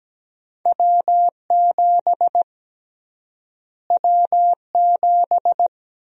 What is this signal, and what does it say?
Morse code, heard as a keyed tone, about 17 wpm, 700 Hz. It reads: W7 W7
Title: USA Call Sign Prefixes - Repeated 2x - 17wpm